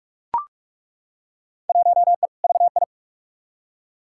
Morse code, heard as a keyed tone, about 42 wpm, 700 Hz. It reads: 1E VI
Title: Field Day Exchanges - 45wpm